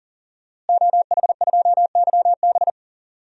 Morse code, heard as a keyed tone, about 40 wpm, 700 Hz. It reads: OH2YB
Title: DXCC-100 Call Signs - 5 Characters - Repeated 2x - 40wpm